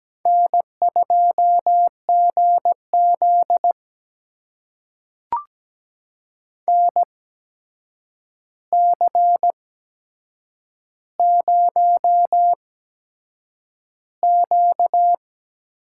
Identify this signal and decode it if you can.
Morse code, heard as a keyed tone, about 17 wpm, spaced as for 7 wpm, 700 Hz. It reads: N2GZ NC0Q
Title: USA Call Signs - 4 Characters - 8x Character Spacing - 17wpm